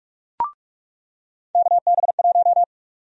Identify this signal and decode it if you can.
Morse code, heard as a keyed tone, about 44 wpm, 700 Hz. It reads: KB1
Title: USA Call Sign Prefixes - 45wpm